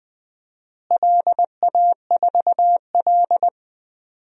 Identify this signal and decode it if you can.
Morse code, heard as a keyed tone, about 20 wpm, 700 Hz. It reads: LA4L
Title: DXCC-100 Call Signs - 4 Characters - Repeated 3x - 20wpm